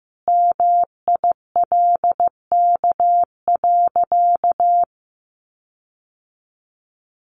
Morse code, heard as a keyed tone, about 15 wpm, 700 Hz. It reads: MILK.
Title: Sentences from Top 1500 Words 15wpm